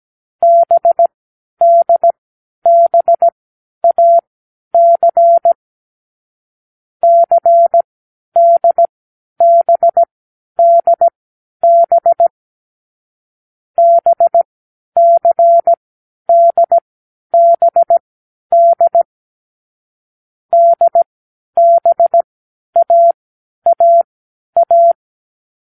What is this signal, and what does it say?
Morse code, heard as a keyed tone, about 17 wpm, spaced as for 10 wpm, 680 Hz. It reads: BDBAC CDBDB BCDBD DBAAA
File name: k7qo / Lesson6(A-D)